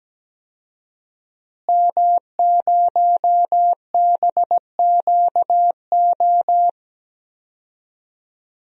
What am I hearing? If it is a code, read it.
Morse code, heard as a keyed tone, about 17 wpm, 700 Hz. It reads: M0BQO
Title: DXCC-100 Call Signs - 5 Characters - Repeated 2x - 17wpm